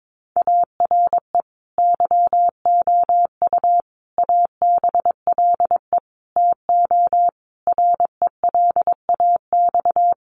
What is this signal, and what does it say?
Morse code, heard as a keyed tone, about 22 wpm, 700 Hz. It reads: ARE YOU ABLE TO RELAX
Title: Sentences from Top 1400 Words 22wpm